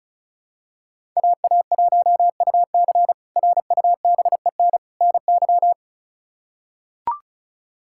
Morse code, heard as a keyed tone, about 36 wpm, 700 Hz. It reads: AA1UC RUBEN NY
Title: K1USN SST QSO Practice - 35wpm